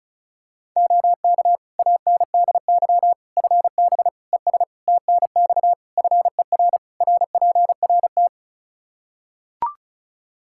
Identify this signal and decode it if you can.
Morse code, heard as a keyed tone, about 36 wpm, 700 Hz. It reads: OK ANDY FB ES TNX FER RPRT
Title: K1USN SST QSO Practice 35wpm